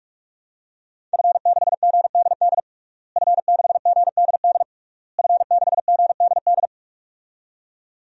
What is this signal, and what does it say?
Morse code, heard as a keyed tone, about 47 wpm, 700 Hz. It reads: F6GDD F6GDD F6GDD
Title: DXCC-100 Call Signs - 5 Characters - Repeated 3x - 45wpm